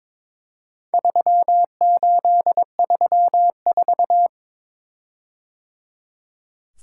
Morse code, heard as a keyed tone, about 22 wpm, 700 Hz. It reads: 3834
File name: practice / 22wpm / Four Letter-Number Combinations - 22wpm